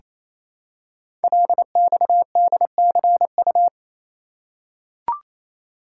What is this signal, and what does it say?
Morse code, heard as a keyed tone, about 27 wpm, 700 Hz. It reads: LXDCU
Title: Code Groups - 5 Letters - 28wpm